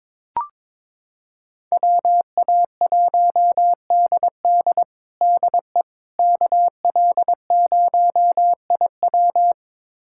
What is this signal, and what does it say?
Morse code, heard as a keyed tone, about 22 wpm, 700 Hz. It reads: WA1DD DE KL0IW
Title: USA Callsign DE Callsign - 5 Characters - 22wpm